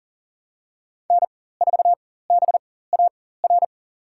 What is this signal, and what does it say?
Morse code, heard as a keyed tone, about 40 wpm, spaced as for 19 wpm, 700 Hz. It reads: N4BAR
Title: USA Call Signs - 5 Characters - 4x Character Spacing - 40wpm